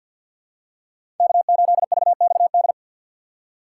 Morse code, heard as a keyed tone, about 48 wpm, 700 Hz. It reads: K8VXD
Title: USA Call Signs - 5 Characters - 4x Character Spacing - 50wpm